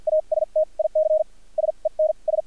Morse code, heard as a keyed tone, about 53 wpm, spaced as for 39 wpm, 620 Hz, with radio noise. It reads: ASTIC SENS
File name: archived / CRSnet